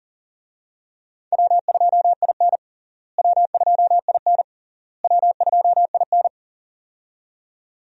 Morse code, heard as a keyed tone, about 40 wpm, 700 Hz. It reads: W2IN W2IN W2IN
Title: USA Call Signs - 4 Characters - Repeated 3x - 40wpm